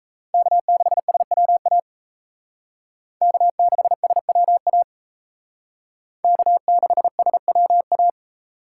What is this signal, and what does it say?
Morse code, heard as a keyed tone, about 35 wpm, 700 Hz. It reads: K6SWA K6SWA K6SWA
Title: Speed Racing - USA 1x3 Call Signs 28wpm